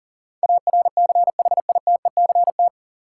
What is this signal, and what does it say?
Morse code, heard as a keyed tone, about 40 wpm, 700 Hz. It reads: ARCHITECT
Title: Practice QSO - Exchange 3 - More Info - 40wpm